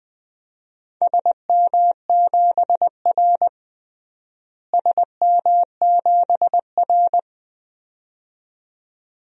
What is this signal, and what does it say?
Morse code, heard as a keyed tone, about 20 wpm, 700 Hz. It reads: SM7R SM7R
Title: DXCC-100 Call Signs - 4 Characters - Repeated 2x - 20wpm